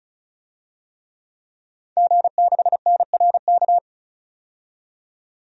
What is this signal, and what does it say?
Morse code, heard as a keyed tone, about 34 wpm, 700 Hz. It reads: G6NRK
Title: DXCC-100 Call Signs - 5 Characters - 35wpm